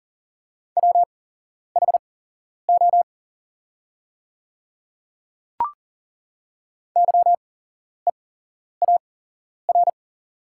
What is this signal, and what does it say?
Morse code, heard as a keyed tone, about 40 wpm, spaced as for 11 wpm, 700 Hz. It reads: WHO YEAR
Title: 8x Character Spacing - Top 300 Words - 40wpm